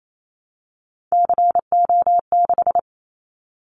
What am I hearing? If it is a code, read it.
Morse code, heard as a keyed tone, about 28 wpm, 700 Hz. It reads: CO6
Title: Callsign Prefixes for Top 100 DXCC - 28wpm